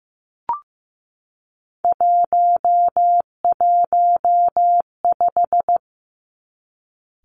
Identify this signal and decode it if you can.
Morse code, heard as a keyed tone, about 15 wpm, 700 Hz. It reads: 115
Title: Three Challenging Letter-Number Combinations 15wpm